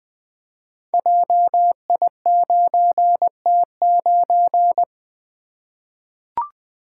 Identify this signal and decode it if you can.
Morse code, heard as a keyed tone, about 20 wpm, 700 Hz. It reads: JI9T9
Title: Code Groups - 5 Letter:Numbers - 20wpm